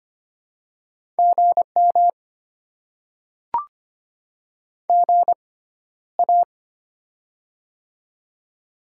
Morse code, heard as a keyed tone, about 25 wpm, spaced as for 13 wpm, 700 Hz. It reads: GM GA
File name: Top 20 QSO Elements - 6x Character Spacing - 25wpm